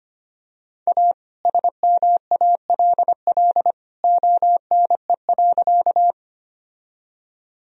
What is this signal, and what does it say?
Morse code, heard as a keyed tone, about 25 wpm, 700 Hz. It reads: A SMALL ONE.
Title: Sentences from Top 1000 Words - Review Part 1 - 25wpm